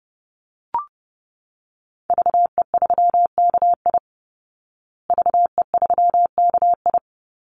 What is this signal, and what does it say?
Morse code, heard as a keyed tone, about 30 wpm, 700 Hz. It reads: VE3KI VE3KI
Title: Call Signs Repeated 2x - 30wpm